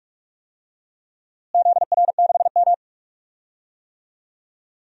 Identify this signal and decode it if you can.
Morse code, heard as a keyed tone, about 46 wpm, 700 Hz. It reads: ZR6M
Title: DXCC-100 Call Signs - 4 Characters - Repeated 2x - 45wpm